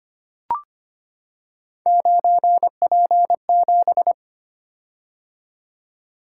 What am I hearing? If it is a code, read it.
Morse code, heard as a keyed tone, about 25 wpm, 700 Hz. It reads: 9P7
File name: Three Letter-Number Combinations - 25wpm